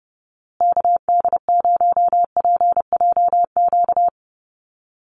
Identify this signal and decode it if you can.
Morse code, heard as a keyed tone, about 30 wpm, 700 Hz. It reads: KD0PJQ
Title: USA Call Signs Repeated 2x 30wpm